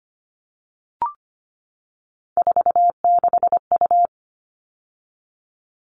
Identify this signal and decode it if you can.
Morse code, heard as a keyed tone, about 25 wpm, 700 Hz. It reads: 46U
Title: All Cut Numbers - 3-Digit 25wpm